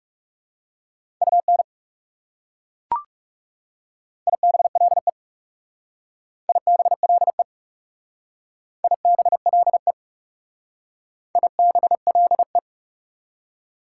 Morse code, heard as a keyed tone, about 36 wpm, 700 Hz. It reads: UN IBLE IBLE IBLE IBLE
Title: Speed-Racing Prefixes and Suffixes 30wpm